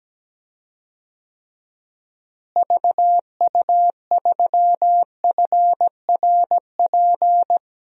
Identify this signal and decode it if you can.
Morse code, heard as a keyed tone, about 17 wpm, 700 Hz. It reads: VU3FRP
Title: DXCC-100 Call Signs - 6 Characters - Repeated 3x - 17wpm